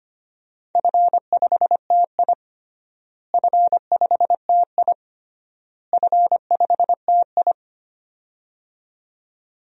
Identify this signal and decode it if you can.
Morse code, heard as a keyed tone, about 25 wpm, 700 Hz. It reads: F5TI F5TI F5TI
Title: DXCC-100 Call Signs - 4 Characters - Repeated 3x - 25wpm